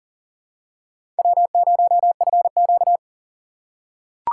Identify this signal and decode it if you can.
Morse code, heard as a keyed tone, about 40 wpm, 700 Hz. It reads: W0FQ
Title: USA Call Signs - 2x Character Spacing - 40wpm